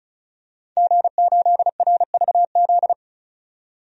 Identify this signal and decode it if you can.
Morse code, heard as a keyed tone, about 34 wpm, 700 Hz. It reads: G8RVZ
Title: DXCC-100 Call Signs - 5 Characters - 35wpm